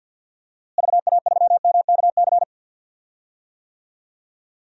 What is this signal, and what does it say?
Morse code, heard as a keyed tone, about 52 wpm, 700 Hz. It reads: VA3MKC
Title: DXCC-100 Call Signs - 6 Characters - Repeated 3x - 50wpm